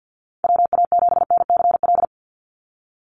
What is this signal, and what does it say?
Morse code, heard as a keyed tone, about 50 wpm, 700 Hz. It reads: WA7NCL
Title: NA SOTA Activator Callsigns 50wpm